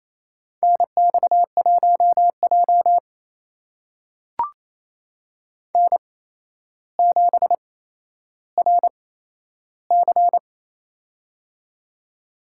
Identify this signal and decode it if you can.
Morse code, heard as a keyed tone, about 28 wpm, spaced as for 12 wpm, 700 Hz. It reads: NX1J N7RC
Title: USA Call Signs - 4 Characters - 8x Character Spacing - 28wpm